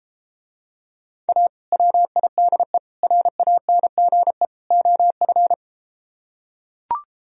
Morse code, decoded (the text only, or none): A WIDE RANGE OF